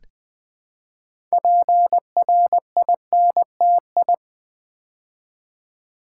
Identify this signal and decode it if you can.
Morse code, heard as a keyed tone, about 20 wpm, 700 Hz. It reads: PRINTI
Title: Build A Word - 2 Letters At A Time - 8-Letter Words - 20wpm